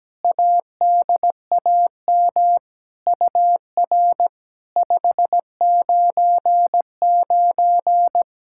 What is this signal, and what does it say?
Morse code, heard as a keyed tone, about 17 wpm, 700 Hz. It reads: ADAM UR 599
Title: SOTA QSO Practice 17wpm